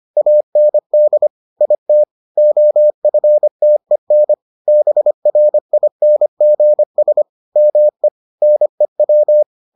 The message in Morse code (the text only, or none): AND IT OFTEN BRINGS ME NEW